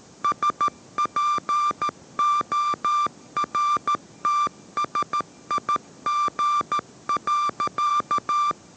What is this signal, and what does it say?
Morse code, heard as a keyed tone, about 13 wpm, 1180 Hz. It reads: SPORTSIG.
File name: 14wpm